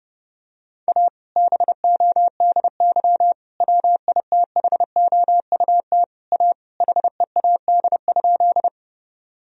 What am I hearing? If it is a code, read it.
Morse code, heard as a keyed tone, about 30 wpm, 700 Hz. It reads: A BODY WITHOUT A HEAD?
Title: Sentences from Top 1000 Words - Review Part 3 - 30wpm